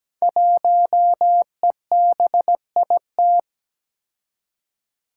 Morse code, heard as a keyed tone, about 17 wpm, 700 Hz. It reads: 1EBIT
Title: Five Challenging Letter-Number Combinations 17wpm